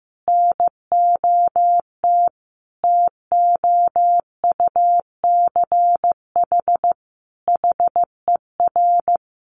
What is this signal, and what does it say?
Morse code, heard as a keyed tone, about 15 wpm, 700 Hz. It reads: NOT TOUCH HER
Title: Sentences from Top 1200 Words 15wpm